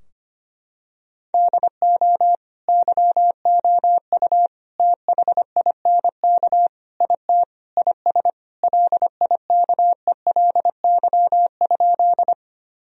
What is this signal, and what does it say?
Morse code, heard as a keyed tone, about 25 wpm, 700 Hz. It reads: DO YOU THINK IT IS LIKELY?
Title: Sentences from Top 1000 Words - Review Part 4 - 25wpm